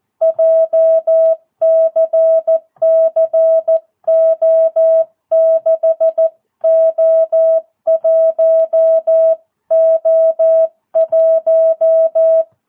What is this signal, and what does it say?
Morse code, heard as a keyed tone, about 14 wpm, 650 Hz. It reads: JCCO6O1O1
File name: jh8cbh / 2022cw